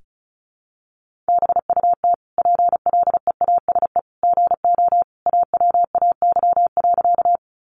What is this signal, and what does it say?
Morse code, heard as a keyed tone, about 34 wpm, 700 Hz. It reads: BUT PLEASE GO AWAY.
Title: Sentences from Top 1500 Words 35wpm